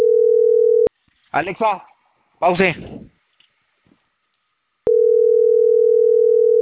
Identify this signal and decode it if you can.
Ringback tone (North America), 440 + 480 Hz, with speech.
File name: CENTRAL CAEA / out agcaea03